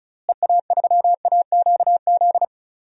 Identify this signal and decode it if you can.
Morse code, heard as a keyed tone, about 36 wpm, 700 Hz. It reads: EA3AQZ
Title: DXCC-100 Call Signs - 6 Characters - Repeated 2x - 35wpm